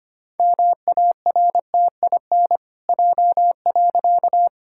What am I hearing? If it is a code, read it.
Morse code, heard as a keyed tone, about 25 wpm, 700 Hz. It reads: MARTIN J.
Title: Pangram Challenge v2 - 25wpm